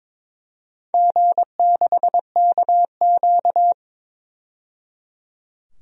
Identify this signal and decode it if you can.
Morse code, heard as a keyed tone, about 22 wpm, 700 Hz. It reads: G6KQ
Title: DXCC-100 Call Signs - 4 Characters - 22wpm